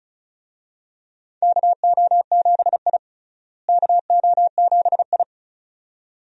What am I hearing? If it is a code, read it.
Morse code, heard as a keyed tone, about 34 wpm, 700 Hz. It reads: KO7I KO7I
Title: USA Call Signs - 4 Characters - Repeated 2x - 35wpm